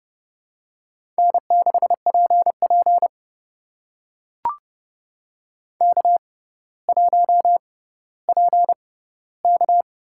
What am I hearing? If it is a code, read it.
Morse code, heard as a keyed tone, about 30 wpm, spaced as for 15 wpm, 690 Hz. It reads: N6PP K1PK
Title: USA Call Signs - 4 Characters - 6x Character Spacing - 30wpm